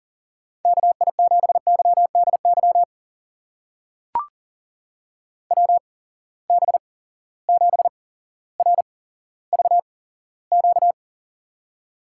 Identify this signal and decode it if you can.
Morse code, heard as a keyed tone, about 39 wpm, spaced as for 17 wpm, 700 Hz. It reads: KI7YDY WB7RVQ
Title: USA Call Signs - 6 Characters - 8x Character Spacing - 40wpm